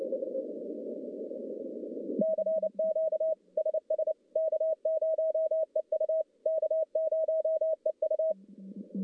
Morse code, heard as a keyed tone, about 28 wpm, 640 Hz. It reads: CQ SS K0EU K0EU
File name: contests / CW-SS-k5tr- -1200